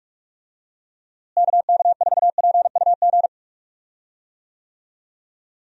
Morse code, heard as a keyed tone, about 46 wpm, 700 Hz. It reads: KK4PUG
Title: USA Call Signs - 6 Characters - Repeated 2x - 45wpm